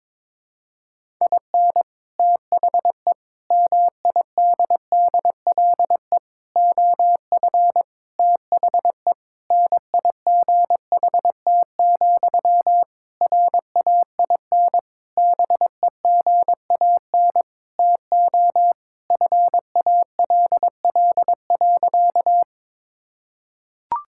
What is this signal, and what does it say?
Morse code, heard as a keyed tone, about 22 wpm, 700 Hz. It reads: IN THE MIDDLE OF THE NIGHT, RAIN BEGAN TO FALL.
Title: Sentences from Top 1000 Words - Review Part 6 - 22wpm